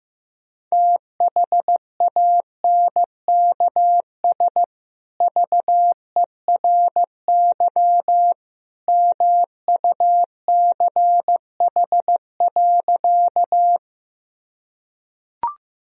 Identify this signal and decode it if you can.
Morse code, heard as a keyed tone, about 15 wpm, 700 Hz. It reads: THANKS VERY MUCH.
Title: Sentences from Top 1000 Words - Review Part 3 - 15wpm